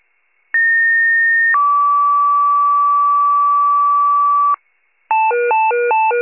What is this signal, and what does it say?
Two-tone page, heard as a one-second tone, then a three-second tone, about 1850 then 1150 Hz, with noise. It is heard from 0.5 to 4.6 s.